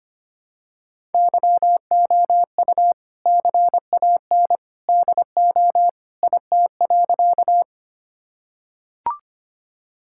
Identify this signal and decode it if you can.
Morse code, heard as a keyed tone, about 25 wpm, 700 Hz. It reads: YOU CAN DO IT.
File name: practice / 25wpm / Sentences with 2-Word Phrases - 25wpm